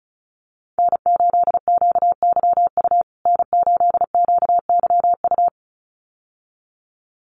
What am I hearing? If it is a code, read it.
Morse code, heard as a keyed tone, about 34 wpm, 700 Hz. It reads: N8QYU N8QYU
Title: USA Call Signs Repeated 2x 35wpm